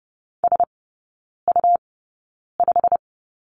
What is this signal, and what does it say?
Morse code, heard as a keyed tone, about 30 wpm, spaced as for 9 wpm, 700 Hz. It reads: SU5
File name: Lesson 18e - Letter m - Three Characters 30wpm